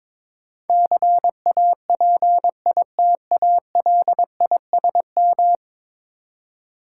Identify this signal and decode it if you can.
Morse code, heard as a keyed tone, about 22 wpm, 700 Hz. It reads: CAPITALISM
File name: practice / 22wpm / 10 Letter Words - First 3 Letters Given - 22wpm